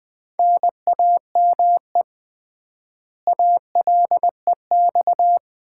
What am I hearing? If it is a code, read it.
Morse code, heard as a keyed tone, about 20 wpm, 700 Hz. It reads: NAME ALEX
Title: Common QSO Phrases - 3x Word-Spacing - 20wpm